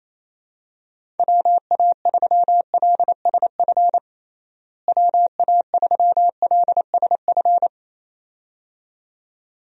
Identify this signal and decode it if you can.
Morse code, heard as a keyed tone, about 28 wpm, 700 Hz. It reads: WA3LSF WA3LSF
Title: USA Call Signs - 6 Characters - Repeated 2x - 28wpm